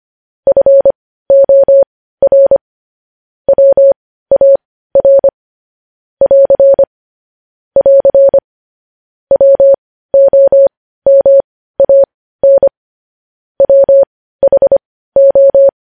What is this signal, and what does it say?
Morse code, heard as a keyed tone, about 25 wpm, spaced as for 15 wpm, 550 Hz. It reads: FOR WAR <AR> <AR> WOMAN WHO
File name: podcast / News headlines-1x-25x15